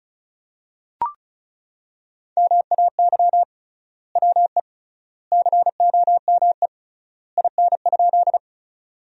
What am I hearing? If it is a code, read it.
Morse code, heard as a keyed tone, about 34 wpm, 700 Hz. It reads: MAY WE COME IN?
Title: 3x Word-Spacing - Sentences from Top 100 Words - 35wpm